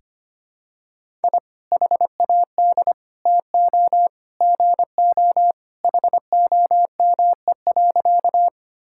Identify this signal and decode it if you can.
Morse code, heard as a keyed tone, about 25 wpm, 700 Hz. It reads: I HAD TO GO HOME.
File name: practice / 25wpm / Sentences with 3-Word Phrases - 25wpm